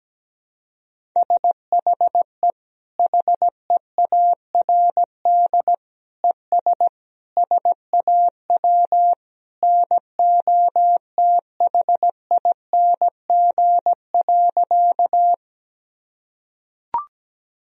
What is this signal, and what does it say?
Morse code, heard as a keyed tone, about 17 wpm, 700 Hz. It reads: SHE HEARD ES SAW NOTHING.
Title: Sentences from Top 500 Words and Abbreviations 17wpm